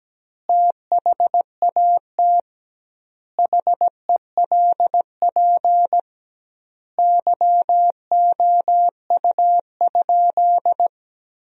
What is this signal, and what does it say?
Morse code, heard as a keyed tone, about 17 wpm, 700 Hz. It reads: THAT HELP YOU?
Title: 2x Word-Spacing - Sentences from Top 200 Words - 17wpm